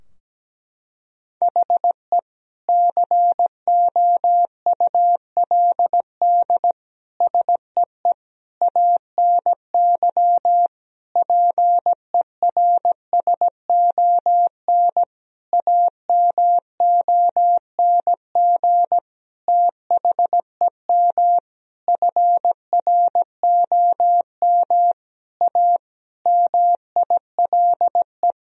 Morse code, heard as a keyed tone, about 17 wpm, 700 Hz. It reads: HE COULD SEE ANY PERSON AMONG THEM FROM A MILE
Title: Sentences from Top 1000 Words - Review Part 3 - 17wpm